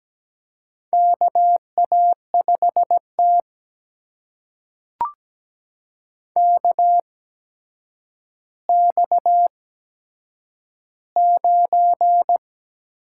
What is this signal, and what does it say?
Morse code, heard as a keyed tone, about 17 wpm, spaced as for 8 wpm, 700 Hz. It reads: KA5T KX9
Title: USA Call Signs - 4 Characters - 8x Character Spacing - 17wpm